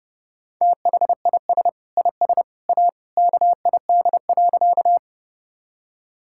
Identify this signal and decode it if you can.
Morse code, heard as a keyed tone, about 32 wpm, 700 Hz. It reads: THIS IS A KID.
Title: Sentences from Top 1600 Words 30wpm